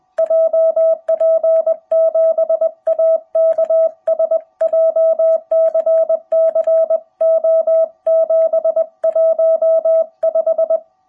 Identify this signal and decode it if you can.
Morse code, heard as a keyed tone, about 20 wpm, 660 Hz. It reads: JP7AKS JCC O715
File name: jh8cbh / 2021cw